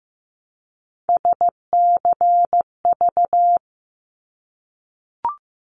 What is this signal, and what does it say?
Morse code, heard as a keyed tone, about 15 wpm, 700 Hz. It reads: SCV
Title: ARRL Sections 15wpm